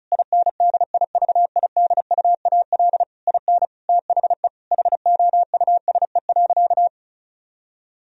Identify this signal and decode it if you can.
Morse code, heard as a keyed tone, about 34 wpm, 700 Hz. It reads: INDIVIDUAL IN THE HOUSE.
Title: Sentences from Top 1000 Words - Review Part 4 - 35wpm